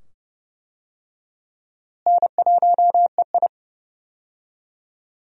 Morse code, heard as a keyed tone, about 29 wpm, 700 Hz. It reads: N1EI
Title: USA Call Signs - 4 Characters - Repeated 2x - 30wpm